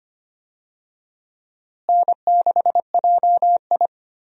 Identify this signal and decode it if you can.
Morse code, heard as a keyed tone, about 25 wpm, 700 Hz. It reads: N6JI
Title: USA Call Signs - 4 Characters - Repeated 3x - 25wpm